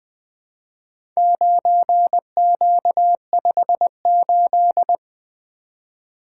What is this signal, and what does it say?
Morse code, heard as a keyed tone, about 21 wpm, 700 Hz. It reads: 9Q58
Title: Four Letter-Number Combinations - 20wpm